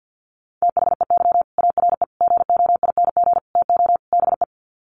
Morse code, heard as a keyed tone, about 50 wpm, 700 Hz. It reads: THEY ARE GOING TO BE
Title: Sets of 5 Words Encore2 50wpm